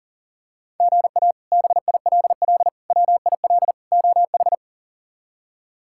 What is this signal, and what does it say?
Morse code, heard as a keyed tone, about 40 wpm, 700 Hz. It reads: GA BILL WIL OH
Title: K1USN SST QSO Practice - 40wpm